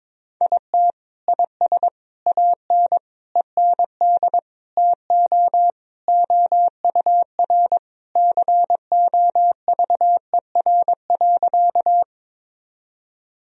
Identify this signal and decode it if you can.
Morse code, heard as a keyed tone, about 22 wpm, 700 Hz. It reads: IT IS AN END TO OUR COVER.
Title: Sentences from Top 1000 Words - Review Part 1 - 22wpm